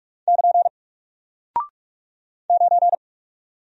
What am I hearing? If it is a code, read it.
Morse code, heard as a keyed tone, about 44 wpm, 700 Hz. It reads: <KN> 9
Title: Single Character (Letter:Number:Punctuation:Prosign) Alt 45wpm